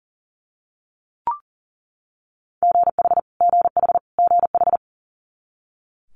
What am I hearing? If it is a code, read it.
Morse code, heard as a keyed tone, about 37 wpm, 700 Hz. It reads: GH GH GH
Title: Multi-letter Phonemes Repeated 3x 40wpm